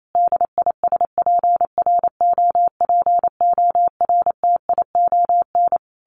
Morse code, heard as a keyed tone, about 28 wpm, 700 Hz. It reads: DISPROPORTION
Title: A Sound Of Thunder 28wpm